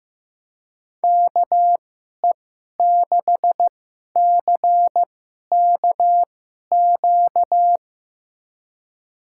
Morse code, heard as a keyed tone, about 16 wpm, spaced as for 11 wpm, 700 Hz. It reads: KE6CKQ
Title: USA Call Signs - 6 Characters - 2x Character Spacing - 15wpm